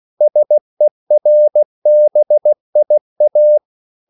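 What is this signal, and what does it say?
Morse code, heard as a keyed tone, about 16 wpm, 600 Hz. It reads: SERBIA
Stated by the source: CW practice of national capitals at 16 words per minute, from ON6ZQ.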